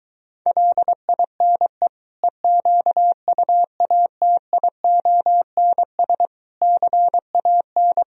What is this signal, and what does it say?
Morse code, heard as a keyed tone, about 22 wpm, 700 Hz. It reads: LINE EQUATIONS CAN